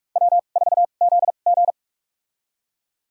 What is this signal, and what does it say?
Morse code, heard as a keyed tone, about 44 wpm, spaced as for 32 wpm, 700 Hz. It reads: W4ZG
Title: USA Call Signs - 4 Characters - 2x Character Spacing - 45wpm